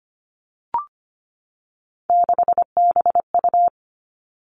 Morse code, heard as a keyed tone, about 25 wpm, 700 Hz. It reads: 6BU
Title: All Cut Numbers - 3-Digit 25wpm